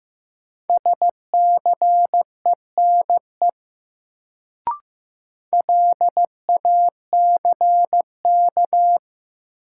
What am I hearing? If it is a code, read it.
Morse code, heard as a keyed tone, about 15 wpm, 700 Hz. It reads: SCENE LACK
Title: Sentences from Top 1100 Words - New Word Drill Alt - 15wpm